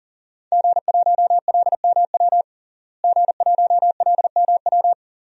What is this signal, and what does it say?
Morse code, heard as a keyed tone, about 40 wpm, 700 Hz. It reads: G1LMW G1LMW
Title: DXCC-100 Call Signs - 5 Characters - Repeated 2x - 40wpm